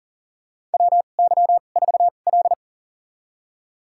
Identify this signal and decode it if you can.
Morse code, heard as a keyed tone, about 38 wpm, spaced as for 28 wpm, 700 Hz. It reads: WY4L
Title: USA Call Signs - 4 Characters - 2x Character Spacing - 40wpm